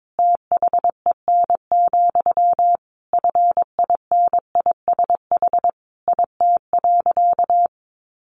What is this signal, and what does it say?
Morse code, heard as a keyed tone, about 22 wpm, 700 Hz. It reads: THEN, FINISH IT.
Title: Sentences from Top 1200 Words 22wpm